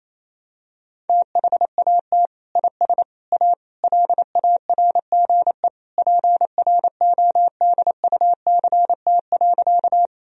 Morse code, heard as a keyed tone, about 28 wpm, 700 Hz. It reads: THAT IS A LARGE PRODUCT.
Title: Sentences from Top 1000 Words - Review Part 2 - 28wpm